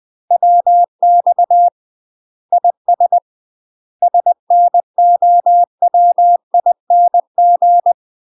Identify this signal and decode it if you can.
Morse code, heard as a keyed tone, about 20 wpm, 700 Hz. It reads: WX IS SNOWING
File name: UK MASTER QSOs TRAINING / 20WPM / UK MASTER QSOs 0042